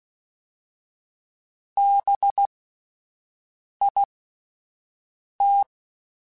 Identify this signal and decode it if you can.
Morse code, heard as a keyed tone, about 16 wpm, spaced as for 5 wpm, 800 Hz. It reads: BIT
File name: Alphabetical / Alphabetical mp3s / Bs @5